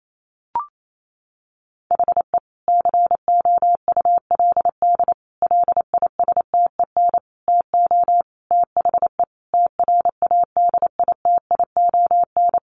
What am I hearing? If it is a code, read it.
Morse code, heard as a keyed tone, about 28 wpm, 700 Hz. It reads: HE COULD LISTEN TO THE TRADITION
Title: Sentences from Top 1500 Words 28wpm